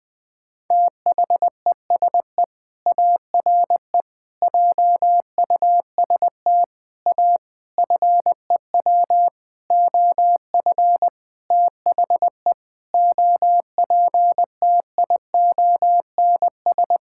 Morse code, heard as a keyed tone, about 21 wpm, 700 Hz. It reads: THESE ARE JUST A FEW OF THE OPTIONS
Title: Sentences with 7-Word Phrases - 20wpm